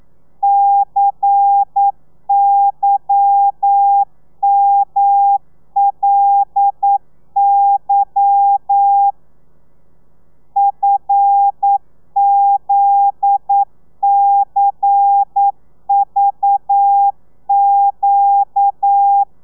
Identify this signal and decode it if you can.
Morse code, heard as a keyed tone, about 9 wpm, 800 Hz. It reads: CYMLY FZCVQ